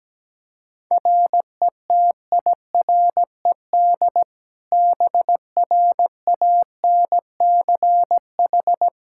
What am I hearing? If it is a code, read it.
Morse code, heard as a keyed tone, about 17 wpm, 700 Hz. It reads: RETIRED BRANCH
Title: Practice QSO - Exchange 3 - More Info - 17wpm